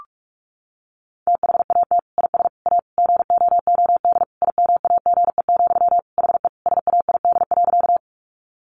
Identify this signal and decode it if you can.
Morse code, heard as a keyed tone, about 44 wpm, 700 Hz. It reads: THAT IS A GOOD IMAGE, HE SAID.
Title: Sentences from Top 700 Words 45wpm